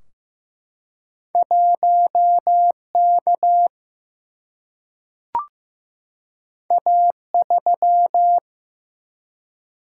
Morse code, heard as a keyed tone, about 15 wpm, 700 Hz. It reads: 1K A3
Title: Single-Letter Single-Number 15wpm